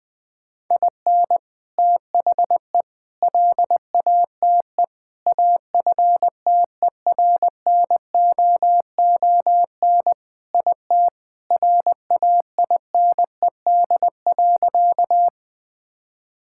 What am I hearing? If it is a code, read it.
Morse code, heard as a keyed tone, about 21 wpm, 700 Hz. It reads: IN THE LATE AFTERNOON IT RAINED.
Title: Sentences with 3-Word Phrases - Spoken Beginning - 20wpm